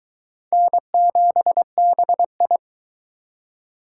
Morse code, heard as a keyed tone, about 22 wpm, 700 Hz. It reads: N7BI